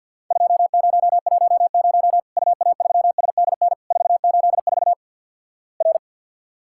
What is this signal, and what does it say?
Morse code, heard as a keyed tone, about 48 wpm, 700 Hz. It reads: 2010 UA3SDN 484 R
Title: Worked All Europe DX Contest - QTC Exchange with Repeats - 50wpm